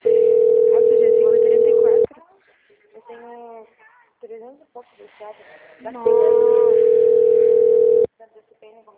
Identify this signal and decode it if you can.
Ringback tone (North America), 440 + 480 Hz, with speech.
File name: out- -2010- - - .363